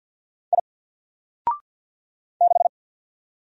Morse code, heard as a keyed tone, about 50 wpm, 700 Hz. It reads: I 6
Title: Single Letter-Number 50wpm